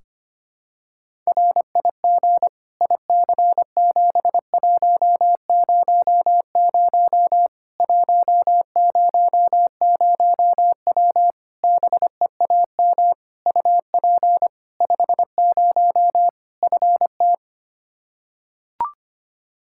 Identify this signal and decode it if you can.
Morse code, heard as a keyed tone, about 25 wpm, 700 Hz. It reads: RIG IC7100 100W BEAM UP 50 FT
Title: Practice QSO - Exchange 2 - Info - 25wpm